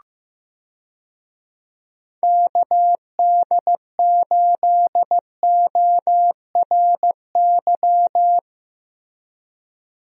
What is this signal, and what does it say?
Morse code, heard as a keyed tone, about 15 wpm, 700 Hz. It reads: KD8ORY
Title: USA Call Signs - 6 Characters - Repeated 2x - 15wpm